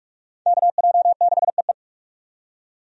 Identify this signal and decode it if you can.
Morse code, heard as a keyed tone, about 45 wpm, 700 Hz. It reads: KJ6EE
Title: USA Call Signs - 5 Characters - 45wpm